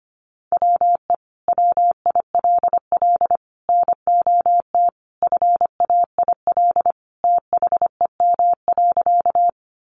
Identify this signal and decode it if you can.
Morse code, heard as a keyed tone, about 25 wpm, 700 Hz. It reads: WE WILL NOT FAIL THEM.
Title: Sentences from Top 1100 Words 25wpm